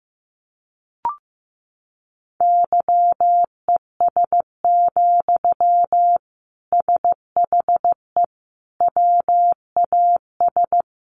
Morse code, heard as a keyed tone, about 15 wpm, 700 Hz. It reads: YES, SHE WAS
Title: Sentences from Top 500 Words 15wpm